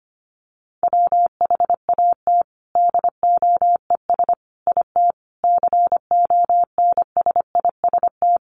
Morse code, heard as a keyed tone, about 25 wpm, 700 Hz. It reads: WHAT DOES IT CONSIST
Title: Sentences from Top 1400 Words 25wpm